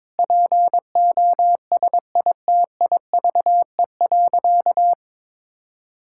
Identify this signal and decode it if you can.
Morse code, heard as a keyed tone, about 22 wpm, 700 Hz. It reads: POSITIVE.
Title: Sentences from Top 1000 Words - Review Part 5 - 22wpm